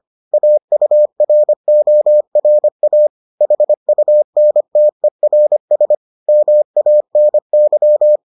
Morse code, heard as a keyed tone, about 25 wpm, 600 Hz. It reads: AURORA HUNTERS MANY